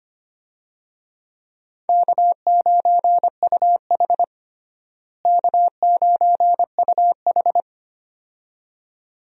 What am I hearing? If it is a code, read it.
Morse code, heard as a keyed tone, about 25 wpm, 700 Hz. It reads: K9UH K9UH
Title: USA Call Signs - 4 Characters - Repeated 2x - 25wpm